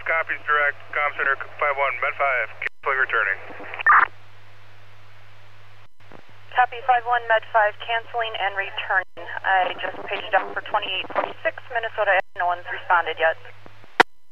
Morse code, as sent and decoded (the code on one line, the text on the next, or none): none